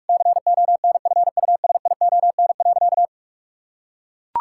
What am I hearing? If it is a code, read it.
Morse code, heard as a keyed tone, about 41 wpm, 700 Hz. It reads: CONFUSION.
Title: Sentences from Top 1600 Words 45wpm